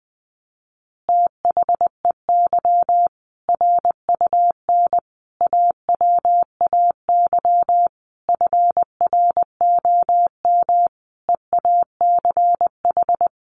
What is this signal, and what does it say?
Morse code, heard as a keyed tone, about 20 wpm, 700 Hz. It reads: THEY RUN AWAY FROM EACH
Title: Easy Phrases 20wpm